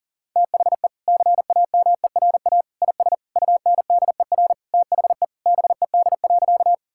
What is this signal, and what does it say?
Morse code, heard as a keyed tone, about 37 wpm, 700 Hz. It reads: THE CAMERA IS UNDER THE BED.
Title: Sentences from Top 1000 Words - Review Part 5 - 40wpm